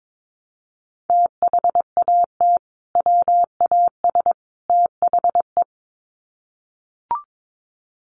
Morse code, heard as a keyed tone, about 22 wpm, 700 Hz. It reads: THAT WAS THE
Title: Sets of 3 Words 22wpm